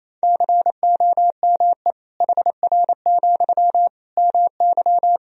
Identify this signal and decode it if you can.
Morse code, heard as a keyed tone, about 28 wpm, 700 Hz. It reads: COME HR, MY
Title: Sentences from Top 300 Words and Abbreviations 28wpm